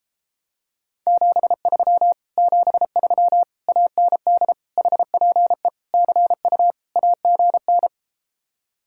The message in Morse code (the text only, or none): 73 73 AND HPE CU AGN